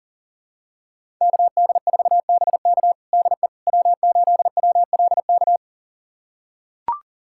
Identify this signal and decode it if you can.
Morse code, heard as a keyed tone, about 40 wpm, 700 Hz. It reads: KD4BK DE W8WLK
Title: USA Callsign DE Callsign 40wpm